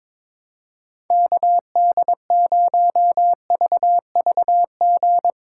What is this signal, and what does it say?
Morse code, heard as a keyed tone, about 22 wpm, 700 Hz. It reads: KD0VVG
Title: USA Call Signs - 6 Characters - 2x Character Spacing - 22wpm